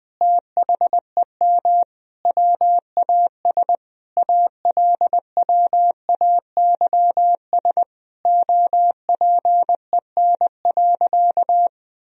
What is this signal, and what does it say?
Morse code, heard as a keyed tone, about 20 wpm, 700 Hz. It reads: THEM WAS ALWAYS OPEN.
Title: Sentences from Top 1000 Words - Review Part 1 - 20wpm